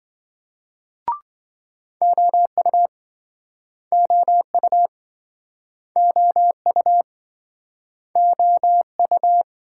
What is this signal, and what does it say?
Morse code, heard as a keyed tone, about 27 wpm, 700 Hz. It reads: OU OU OU OU